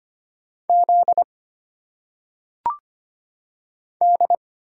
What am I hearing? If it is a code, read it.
Morse code, heard as a keyed tone, about 25 wpm, 700 Hz. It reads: Z D
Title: Single Letters - Phonetic Pronunciation 25wpm